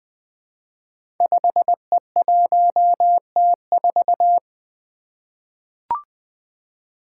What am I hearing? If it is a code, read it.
Morse code, heard as a keyed tone, about 20 wpm, 700 Hz. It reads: 5E1T4
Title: Five Challenging Letter-Number Combinations 20wpm